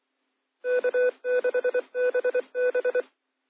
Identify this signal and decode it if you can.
Morse code, heard as a keyed tone, about 24 wpm, 500 Hz. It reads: K6BB